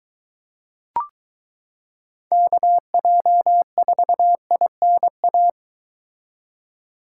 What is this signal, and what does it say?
Morse code, heard as a keyed tone, about 23 wpm, 700 Hz. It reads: KJ4INA